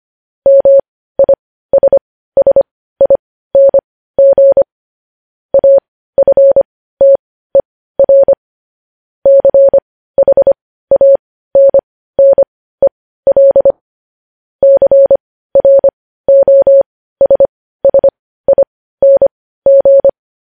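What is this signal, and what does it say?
Morse code, heard as a keyed tone, about 25 wpm, spaced as for 15 wpm, 550 Hz. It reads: MISSING AFTER CHANNEL CROSSING
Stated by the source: Morse code news headlines sent at 15, preceded by vvv vvv.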